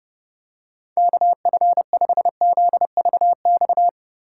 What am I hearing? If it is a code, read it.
Morse code, heard as a keyed tone, about 32 wpm, 700 Hz. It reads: KF5ZVX
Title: USA Call Signs - 3x Character Spacing - 30wpm